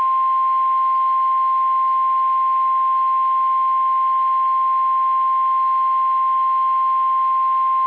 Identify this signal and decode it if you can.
Emergency Alert System (SAME) tones (the 1050 Hz attention tone). It is heard from 1.0 to 7.1 s.